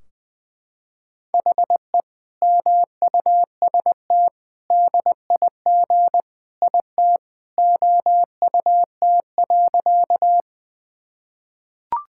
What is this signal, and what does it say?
Morse code, heard as a keyed tone, about 20 wpm, 700 Hz. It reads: HE MUST DIG IT OUT.
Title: Sentences from Top 1600 Words 20wpm